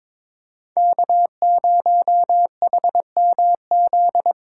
Morse code, heard as a keyed tone, about 22 wpm, 700 Hz. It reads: K0HMZ
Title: K1USN SST QSO Practice - 22wpm